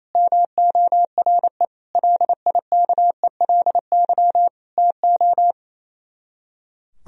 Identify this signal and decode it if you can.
Morse code, heard as a keyed tone, about 34 wpm, 700 Hz. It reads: MORE LIKELY TO
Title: Speed-Racing - Sets of 4 Words 28wpm